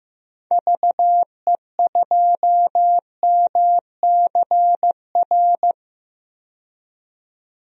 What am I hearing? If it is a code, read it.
Morse code, heard as a keyed tone, about 15 wpm, 700 Hz. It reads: VE2MCR
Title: DXCC-100 Call Signs - 6 Characters - 15wpm